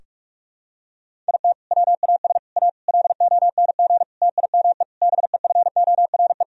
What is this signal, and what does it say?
Morse code, heard as a keyed tone, about 46 wpm, 700 Hz. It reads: IT WAS A LONG TIME BEFORE
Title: Sentences from Top 1000 Words - Review Part 2 - 45wpm